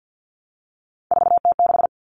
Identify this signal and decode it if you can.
Morse code, heard as a keyed tone, about 50 wpm, 700 Hz. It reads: 4T6
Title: Common Cut Numbers - 3-Digit 50wpm